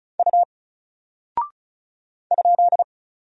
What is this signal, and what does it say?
Morse code, heard as a keyed tone, about 36 wpm, 700 Hz. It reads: U ?
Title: Single Character (Letter:Number:Punctuation:Prosign) 35wpm